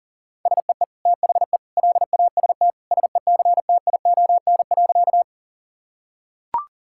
Morse code, heard as a keyed tone, about 40 wpm, 700 Hz. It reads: SEE THE LAST SECTION.
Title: Sentences from Top 1000 Words - Review Part 5 - 40wpm